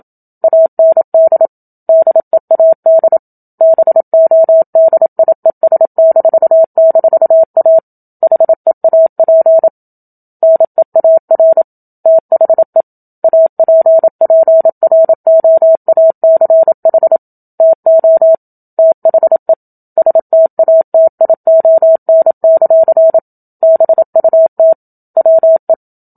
Morse code, heard as a keyed tone, about 27 wpm, 650 Hz. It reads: AND DEAD BODIES--A HEAP NEAR THE APPROACH TO THE STATION; BUT WE
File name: Book Two / 01-Chapter 1 - 27 WPM